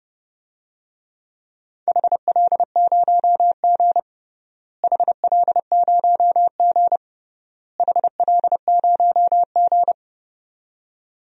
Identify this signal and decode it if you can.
Morse code, heard as a keyed tone, about 32 wpm, 700 Hz. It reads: HL0G HL0G HL0G
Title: DXCC-100 Call Signs - 4 Characters - Repeated 3x - 30wpm